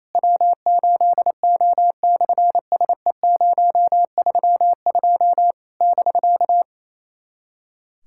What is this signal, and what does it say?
Morse code, heard as a keyed tone, about 27 wpm, 700 Hz. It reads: W8O/SE032 <BK>
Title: SOTA QSO Practice 28wpm